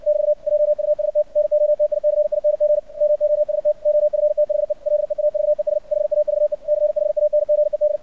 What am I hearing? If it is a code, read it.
Morse code, heard as a keyed tone, about 64 wpm, 600 Hz. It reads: <BT> JUST NOTICING MOST OF THE LEAVES HAVE CHANGED
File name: archived / CRSnet